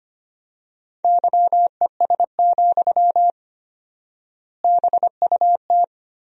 Morse code, heard as a keyed tone, about 25 wpm, 700 Hz. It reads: YES, BUT
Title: 4x Word-Spacing - Sentences from Top 500 Words - 25wpm